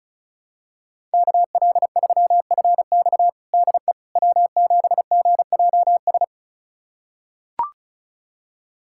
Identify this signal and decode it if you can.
Morse code, heard as a keyed tone, about 36 wpm, 700 Hz. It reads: KL3FX DE W7GJS
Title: USA Callsign DE Callsign - 5 Characters - 35wpm